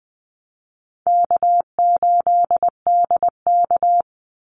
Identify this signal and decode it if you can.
Morse code, heard as a keyed tone, about 20 wpm, 700 Hz. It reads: K8DK
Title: Easy Call Signs Repeated 3x - 20wpm